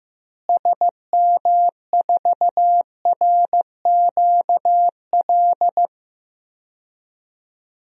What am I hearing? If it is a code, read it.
Morse code, heard as a keyed tone, about 16 wpm, 700 Hz. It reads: SM4RQL
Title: DXCC-100 Call Signs - 6 Characters - Repeated 2x - 15wpm